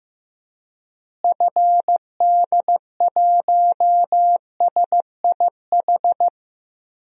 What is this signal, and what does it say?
Morse code, heard as a keyed tone, about 15 wpm, 700 Hz. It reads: FD1SIH
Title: DXCC-100 Call Signs - 6 Characters - Repeated 3x - 15wpm